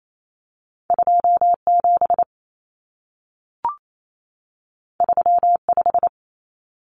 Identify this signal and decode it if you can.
Morse code, heard as a keyed tone, about 28 wpm, 700 Hz. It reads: 27 35
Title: Common Cut Numbers - 2-Digit 28wpm